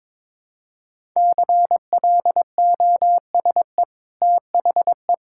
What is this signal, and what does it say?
Morse code, heard as a keyed tone, about 22 wpm, 690 Hz. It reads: CLOSE THE
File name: practice / 22wpm / Easy Sentences with Hints - 22wpm